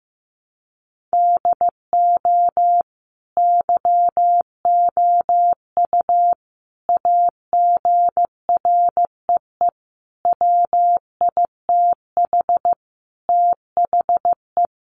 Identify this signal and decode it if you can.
Morse code, heard as a keyed tone, about 15 wpm, 700 Hz. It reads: DO YOU AGREE WITH THE
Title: Sentences from Top 1400 Words 15wpm